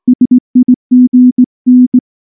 Morse code, heard as a keyed tone, about 19 wpm, 250 Hz. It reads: SIGN
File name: morse / morse-training-20WPM-5